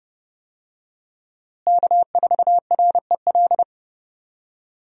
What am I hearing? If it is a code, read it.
Morse code, heard as a keyed tone, about 30 wpm, 700 Hz. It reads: K4REL
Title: USA Call Signs - 5 Characters - Repeated 3x - 30wpm